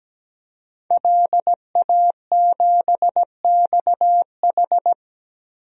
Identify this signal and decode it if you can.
Morse code, heard as a keyed tone, about 17 wpm, 700 Hz. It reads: LA7XH
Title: DXCC-100 Call Signs - 5 Characters - Repeated 3x - 17wpm